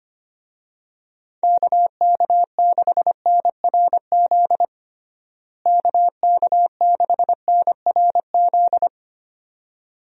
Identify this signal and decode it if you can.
Morse code, heard as a keyed tone, about 25 wpm, 700 Hz. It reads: KK6NRZ KK6NRZ
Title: USA Call Signs - 6 Characters - Repeated 2x - 25wpm